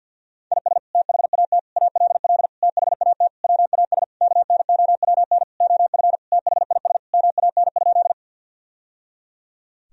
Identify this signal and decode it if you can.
Morse code, heard as a keyed tone, about 51 wpm, 700 Hz. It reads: IS THAT ALL THAT WAS KNOWN OF THIS MAN?
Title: Sentences from Top 1000 Words - Review Part 3 - 50wpm